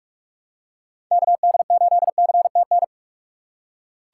Morse code, heard as a keyed tone, about 44 wpm, 700 Hz. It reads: KD8CTN
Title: USA Call Signs - 6 Characters - 3x Character Spacing - 45wpm